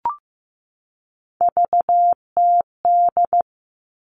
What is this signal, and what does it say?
Morse code, heard as a keyed tone, about 15 wpm, 700 Hz. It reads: VTD
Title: All Cut Numbers - 3-Digit 15wpm